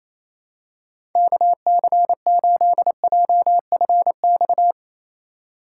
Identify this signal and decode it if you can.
Morse code, heard as a keyed tone, about 27 wpm, 700 Hz. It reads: KC8JFX
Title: USA Call Signs - 6 Characters - 2x Character Spacing - 28wpm